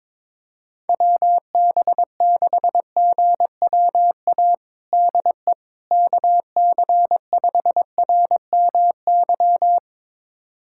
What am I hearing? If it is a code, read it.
Morse code, heard as a keyed tone, about 22 wpm, 700 Hz. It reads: WB6GWA DE KC5RMY
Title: USA Callsign DE Callsign 22wpm